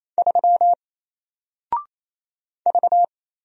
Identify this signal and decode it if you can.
Morse code, heard as a keyed tone, about 27 wpm, 700 Hz. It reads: 3 V
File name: Single-Letter-Number-28wpm